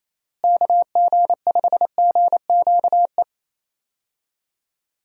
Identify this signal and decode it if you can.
Morse code, heard as a keyed tone, about 28 wpm, 700 Hz. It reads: KG5GQE
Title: USA Call Signs - 6 Characters - Repeated 3x - 28wpm